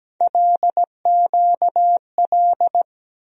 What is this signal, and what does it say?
Morse code, heard as a keyed tone, about 17 wpm, 700 Hz. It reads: LQL
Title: USA Call Sign Postfixes - 3 Letters 17wpm